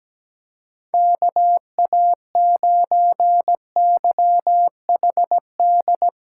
Morse code, heard as a keyed tone, about 17 wpm, 700 Hz. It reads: KA9YHD
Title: USA Callsign DE Callsign 17wpm